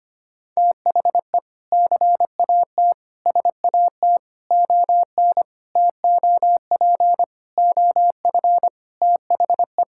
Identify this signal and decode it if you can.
Morse code, heard as a keyed tone, about 25 wpm, 700 Hz. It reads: THE CAT SAT ON TOP OF THE
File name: practice / 25wpm / Sentences with 4-Word Phrases - 25wpm